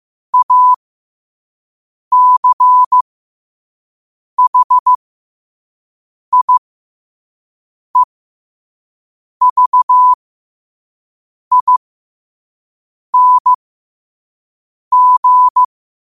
Morse code, heard as a keyed tone, about 15 wpm, spaced as for 5 wpm, 1000 Hz. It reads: ACHIEVING
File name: qotdfiles / 5wpm